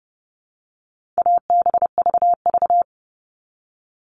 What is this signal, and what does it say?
Morse code, heard as a keyed tone, about 30 wpm, 700 Hz. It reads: ABVV
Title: All Cut Numbers - 4-Digit 30wpm